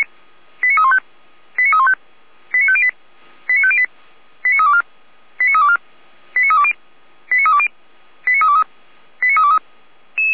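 Five-tone selective call (ZVEI), heard as quick runs of five tones, tones of about 70 ms. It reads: 89315 89315 89508 89508 89324 89324 89320 89320 89323 89323
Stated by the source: ZVEI five-tone selective call received by SDR.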